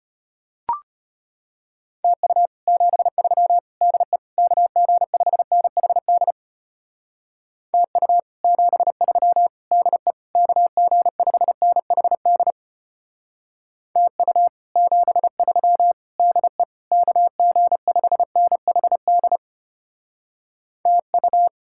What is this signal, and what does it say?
Morse code, heard as a keyed tone, about 31 wpm, 700 Hz. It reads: TU 73 DE KG5NHD TU 73 DE KG5NHD TU 73 DE KG5NHD TU
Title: Speed-Racing Common QSO Phrases - 25wpm